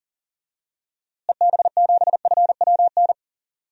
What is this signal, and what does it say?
Morse code, heard as a keyed tone, about 40 wpm, 700 Hz. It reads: EB7FWN
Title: DXCC-100 Call Signs - 6 Characters - Repeated 2x - 40wpm